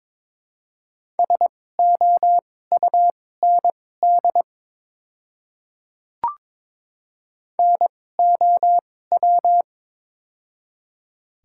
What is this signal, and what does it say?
Morse code, heard as a keyed tone, about 22 wpm, spaced as for 16 wpm, 700 Hz. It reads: SOUND NOW
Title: 2x Character Spacing - Top 100 Words - 22wpm